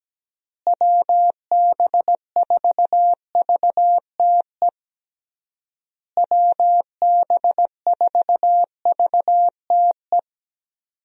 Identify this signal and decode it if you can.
Morse code, heard as a keyed tone, about 17 wpm, 700 Hz. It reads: WB4VTE WB4VTE
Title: USA Call Signs - 6 Characters - Repeated 3x - 17wpm